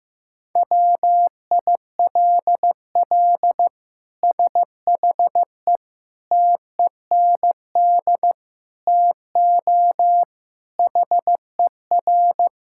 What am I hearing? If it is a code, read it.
Morse code, heard as a keyed tone, about 15 wpm, 700 Hz. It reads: WILL SHE TEND TO HER
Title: Sentences from Top 1000 Words - Review Part 5 - 15wpm